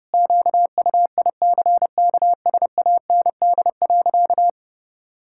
Morse code, heard as a keyed tone, about 30 wpm, 700 Hz. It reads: QUICKSAND.
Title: Pangram Challenge v2 - 30wpm